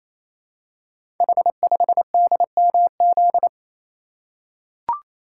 Morse code, heard as a keyed tone, about 27 wpm, 700 Hz. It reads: H5DMZ
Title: Code Groups - 5 Letter:Numbers - 28wpm